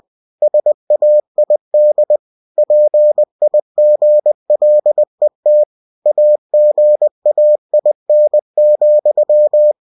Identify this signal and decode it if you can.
Morse code, heard as a keyed tone, about 20 wpm, 600 Hz. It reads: SAID PIGLET AGAIN,
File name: cw-books / winnie-the-pooh / winnie-20wpm-page-0071